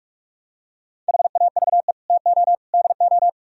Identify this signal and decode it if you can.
Morse code, heard as a keyed tone, about 44 wpm, 700 Hz. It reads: HAVE TO DO